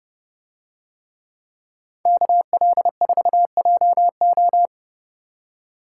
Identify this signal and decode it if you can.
Morse code, heard as a keyed tone, about 30 wpm, 700 Hz. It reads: KL4JO